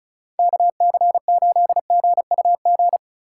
Morse code, heard as a keyed tone, about 34 wpm, 700 Hz. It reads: KC8GUG
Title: USA Call Signs - 4x Character Spacing - 35wpm